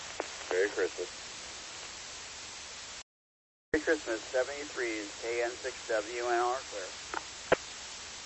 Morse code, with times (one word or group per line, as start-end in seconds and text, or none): none